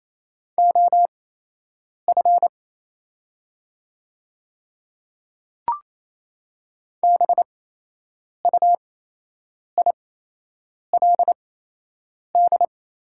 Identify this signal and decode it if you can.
Morse code, heard as a keyed tone, about 28 wpm, spaced as for 8 wpm, 700 Hz. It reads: OF BUILD
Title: 8x Character Spacing - Top 300 Words - 28wpm